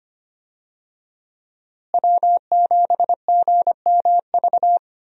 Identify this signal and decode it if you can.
Morse code, heard as a keyed tone, about 25 wpm, 700 Hz. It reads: W7GMV
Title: USA Call Signs - 5 Characters - Repeated 2x - 25wpm